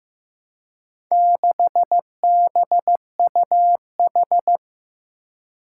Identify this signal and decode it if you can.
Morse code, heard as a keyed tone, about 15 wpm, 700 Hz. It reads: 6BUH
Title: Four Challenging Letter-Number Combinations 15wpm